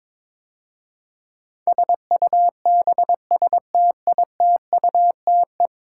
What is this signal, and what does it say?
Morse code, heard as a keyed tone, about 22 wpm, 700 Hz. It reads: SUBSTITUTE
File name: practice / 22wpm / 10 Letter Words - First 3 Letters Given - 22wpm